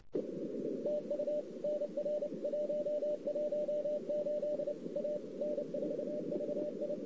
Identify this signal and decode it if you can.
Morse code, heard as a keyed tone, about 29 wpm, 590 Hz, with radio noise. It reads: TU NR 118 A N4VI
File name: CW / 2008-ss-cw-k5tr-op-n5rz -00:00